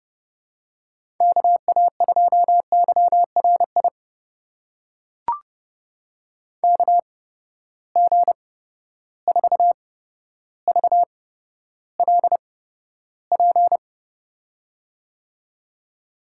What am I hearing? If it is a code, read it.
Morse code, heard as a keyed tone, about 30 wpm, spaced as for 13 wpm, 700 Hz. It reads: KA2YRI KG4VLP
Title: USA Call Signs - 6 Characters - 8x Character Spacing - 30wpm